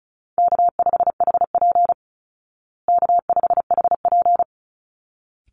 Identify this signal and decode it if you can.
Morse code, heard as a keyed tone, about 34 wpm, 700 Hz. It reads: K5HP K5HP
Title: Call Signs Repeated 3x - 35wpm